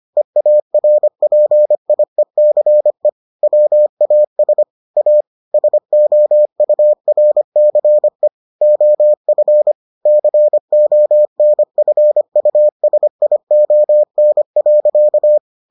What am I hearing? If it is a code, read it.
Morse code, heard as a keyed tone, about 25 wpm, 600 Hz. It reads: EARPIECE WAS A SOURCE OF CONFUSION.